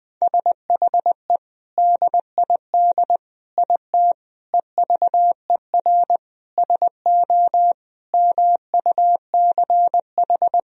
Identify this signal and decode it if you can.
Morse code, heard as a keyed tone, about 20 wpm, 700 Hz. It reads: SHE DID IT EVER SO MUCH
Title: Sentences from Top 400 Words and Abbreviations 20wpm